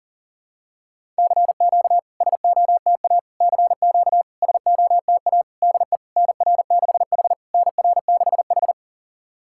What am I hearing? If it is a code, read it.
Morse code, heard as a keyed tone, about 40 wpm, 690 Hz. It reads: CQ SOTA CQ SOTA DE NR6H NR6H
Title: SOTA QSO Practice 40wpm